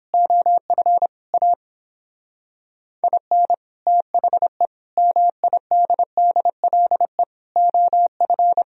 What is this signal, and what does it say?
Morse code, heard as a keyed tone, about 29 wpm, 700 Hz. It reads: OF A IN THE MIDDLE OF